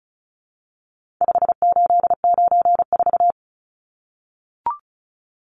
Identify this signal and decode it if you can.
Morse code, heard as a keyed tone, about 34 wpm, 700 Hz. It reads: 5894
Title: Numbers - 4-Digit 35wpm